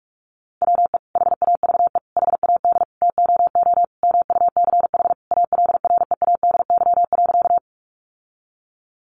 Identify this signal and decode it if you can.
Morse code, heard as a keyed tone, about 44 wpm, 700 Hz. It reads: WE HAVE HAD TOO MUCH ALREADY.
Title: Sentences from Top 700 Words 45wpm